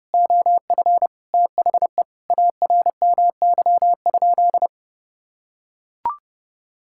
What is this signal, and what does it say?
Morse code, heard as a keyed tone, about 30 wpm, 700 Hz. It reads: OF THE ARMY?
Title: Sentences from Top 1000 Words - Review Part 6 - 30wpm